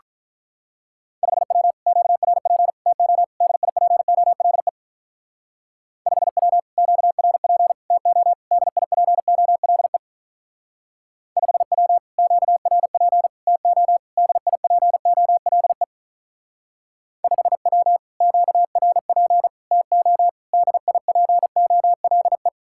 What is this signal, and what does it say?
Morse code, heard as a keyed tone, about 45 wpm, 700 Hz. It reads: 5W QRP TO DIPOLE 5W QRP TO DIPOLE 5W QRP TO DIPOLE 5W QRP TO DIPOLE